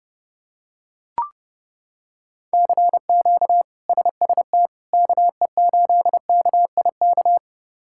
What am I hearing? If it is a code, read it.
Morse code, heard as a keyed tone, about 30 wpm, 700 Hz. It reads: CQ SST KE8KIK
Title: K1USN SST QSO Practice - 30wpm